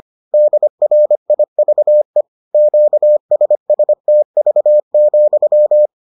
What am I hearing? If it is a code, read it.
Morse code, heard as a keyed tone, about 25 wpm, 600 Hz. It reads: DRIVE QSSTV,